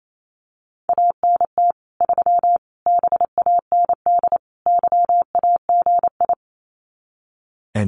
Morse code, heard as a keyed tone, about 28 wpm, 700 Hz. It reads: ANT 3 BAND YAGI
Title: Common QSO Phrases 28wpm